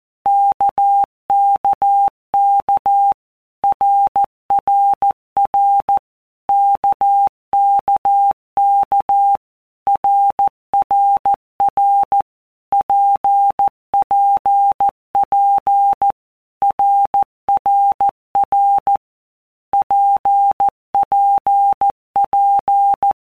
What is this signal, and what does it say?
Morse code, heard as a keyed tone, about 14 wpm, 800 Hz. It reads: KKK RRR KKK RRR PPP RRR PPP
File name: All Groups 1-minute Blocks / Groups@13 / RKPX